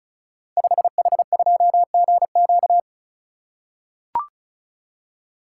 Morse code, heard as a keyed tone, about 35 wpm, 700 Hz. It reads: 5H2GQ
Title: Code Groups - 5 Letter:Numbers - 35wpm